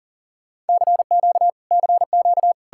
Morse code, heard as a keyed tone, about 60 wpm, spaced as for 50 wpm, 700 Hz. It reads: CQ CQ
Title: Speed-Racing Common QSO Phrases - 40wpm